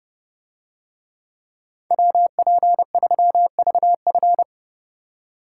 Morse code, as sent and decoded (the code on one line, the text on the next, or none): .-- .--. ...-- ...- ..-.
WP3VF